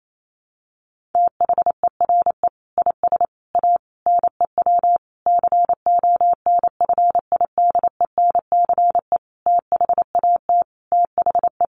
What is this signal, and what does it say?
Morse code, heard as a keyed tone, about 28 wpm, 700 Hz. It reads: THERE IS A NEW CONFIDENCE THAT THE
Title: Sentences from Top 1500 Words 28wpm